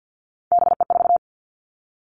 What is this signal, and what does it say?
Morse code, heard as a keyed tone, about 50 wpm, 700 Hz. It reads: BE4
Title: Three Challenging Letter-Number Combinations 50wpm